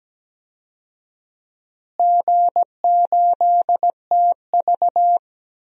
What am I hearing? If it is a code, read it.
Morse code, heard as a keyed tone, about 17 wpm, 700 Hz. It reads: G8TV
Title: DXCC-100 Call Signs - 4 Characters - Repeated 2x - 17wpm